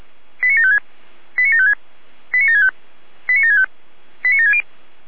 Five-tone selective call (ZVEI), heard as quick runs of five tones, tones of about 80 ms. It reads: 89756 89756 89765 89765 89760
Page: SDR/ZVEI